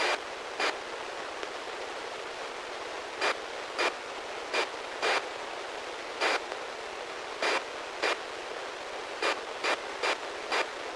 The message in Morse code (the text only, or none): none